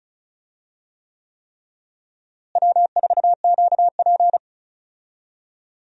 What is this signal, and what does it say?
Morse code, heard as a keyed tone, about 36 wpm, 700 Hz. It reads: W4QP
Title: USA Call Signs - 4 Characters - Repeated 3x - 35wpm